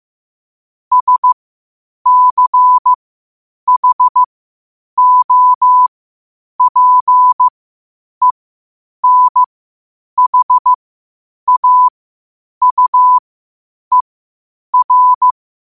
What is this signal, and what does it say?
Morse code, heard as a keyed tone, about 15 wpm, spaced as for 8 wpm, 1000 Hz. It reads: SCHOPENHAUER
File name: qotdfiles / 8wpm